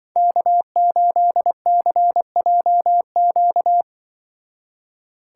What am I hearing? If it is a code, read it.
Morse code, heard as a keyed tone, about 23 wpm, 700 Hz. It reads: K8CJQ
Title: Speed Racing - USA Call Signs 20wpm